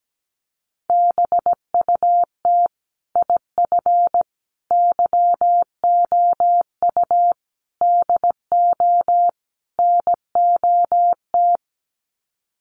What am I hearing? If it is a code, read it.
Morse code, heard as a keyed tone, about 17 wpm, 700 Hz. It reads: BUT IF YOU DO NOT
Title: Sets of 5 Words Encore 17wpm